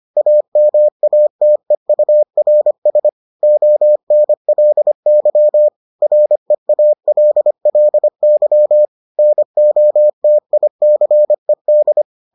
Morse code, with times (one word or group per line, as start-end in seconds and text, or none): 0.2-3.1 AMATEURS
3.4-5.7 ONLY
6.0-8.9 REALLY
9.2-12.0 NOTICED